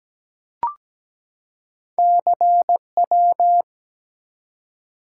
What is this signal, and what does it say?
Morse code, heard as a keyed tone, about 17 wpm, 700 Hz. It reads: CW
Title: Top 180 QSO Elements 17wpm